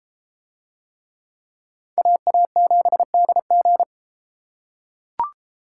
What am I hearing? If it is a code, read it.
Morse code, heard as a keyed tone, about 33 wpm, 700 Hz. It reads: AA7DG